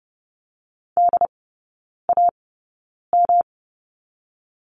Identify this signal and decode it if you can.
Morse code, heard as a keyed tone, about 30 wpm, spaced as for 9 wpm, 700 Hz. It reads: DAM
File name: Lesson 18e - Letter m - Three Characters 30wpm